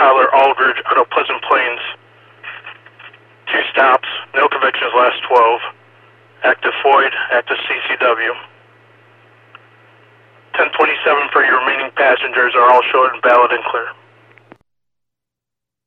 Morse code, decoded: none

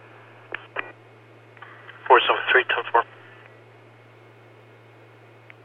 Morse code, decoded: none

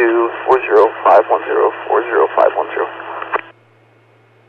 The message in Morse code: none

